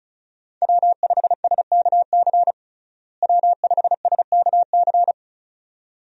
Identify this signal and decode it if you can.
Morse code, heard as a keyed tone, about 36 wpm, 700 Hz. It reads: W5SKC W5SKC
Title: USA Call Signs - 5 Characters - Repeated 2x - 35wpm